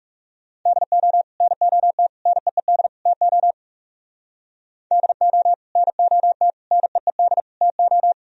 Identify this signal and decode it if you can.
Morse code, heard as a keyed tone, about 36 wpm, 700 Hz. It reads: DO NOT NEED TO DO NOT NEED TO
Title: Speed-Racing - Sets of 4 Words 30wpm